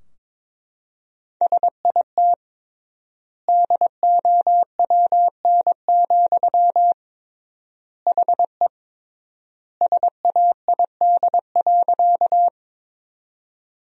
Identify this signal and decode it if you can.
Morse code, heard as a keyed tone, about 22 wpm, 700 Hz. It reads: SIT DOWN, HE SAID.
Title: 3x Word-Spacing - Sentences from Top 500 Words - 22wpm